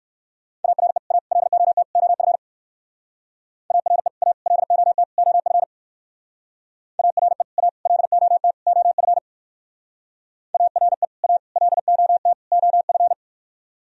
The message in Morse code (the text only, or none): ARE A LOT OF ARE A LOT OF ARE A LOT OF ARE A LOT OF